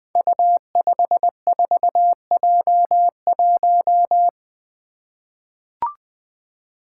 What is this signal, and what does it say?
Morse code, heard as a keyed tone, about 20 wpm, 700 Hz. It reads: U54J1
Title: Five Challenging Letter-Number Combinations 20wpm